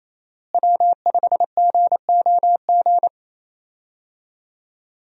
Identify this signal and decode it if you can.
Morse code, heard as a keyed tone, about 28 wpm, 700 Hz. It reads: W5GOG
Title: USA Call Signs - 5 Characters - Repeated 2x - 28wpm